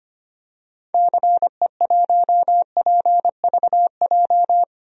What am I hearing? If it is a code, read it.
Morse code, heard as a keyed tone, about 25 wpm, 700 Hz. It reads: CE1PVJ
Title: DXCC-100 Call Signs - 6 Characters - Repeated 3x - 25wpm